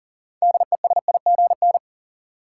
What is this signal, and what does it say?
Morse code, heard as a keyed tone, about 40 wpm, 700 Hz. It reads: DESIGN
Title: Sentences from Top 800 Words - New Word Drill - 40wpm